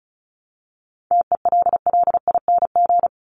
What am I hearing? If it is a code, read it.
Morse code, heard as a keyed tone, about 34 wpm, 700 Hz. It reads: TELLING
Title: ING Suffix - Top 100 Words with Base Words 35wpm